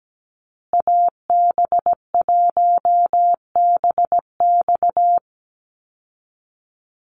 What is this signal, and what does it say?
Morse code, heard as a keyed tone, about 17 wpm, 700 Hz. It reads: AB1BX
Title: Call Signs Repeated 3x - 17wpm